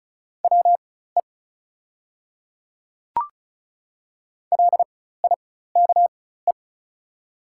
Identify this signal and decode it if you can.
Morse code, heard as a keyed tone, about 36 wpm, spaced as for 16 wpm, 690 Hz. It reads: WE LIKE
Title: 4x Character Spacing - Top 200 Words - 35wpm